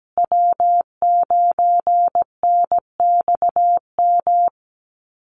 Morse code, heard as a keyed tone, about 17 wpm, 700 Hz. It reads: W9NXM
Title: Call Signs Repeated 2x - 17wpm